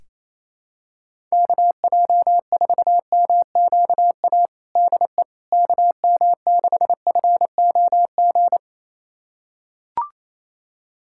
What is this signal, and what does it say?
Morse code, heard as a keyed tone, about 28 wpm, 700 Hz. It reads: KJ4MQA DE KM6FOG
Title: USA Callsign DE Callsign - 6 Characters - 28wpm